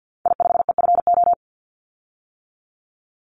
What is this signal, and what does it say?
Morse code, heard as a keyed tone, about 49 wpm, 700 Hz. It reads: I5EFO
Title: Call Signs - 50wpm